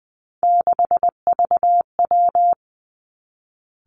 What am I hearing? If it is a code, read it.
Morse code, heard as a keyed tone, about 20 wpm, 700 Hz. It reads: 6VW
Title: Three Challenging Letter-Number Combinations 20wpm